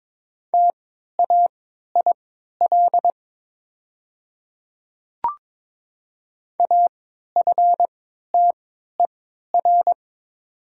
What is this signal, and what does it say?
Morse code, heard as a keyed tone, about 22 wpm, spaced as for 12 wpm, 700 Hz. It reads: TAIL AFTER
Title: 3x Character Spacing - Top 500 Words - 22wpm